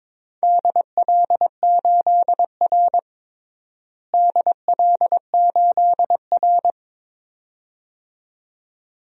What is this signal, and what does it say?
Morse code, heard as a keyed tone, about 22 wpm, 700 Hz. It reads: DL8R DL8R
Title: DXCC-100 Call Signs - 4 Characters - Repeated 2x - 22wpm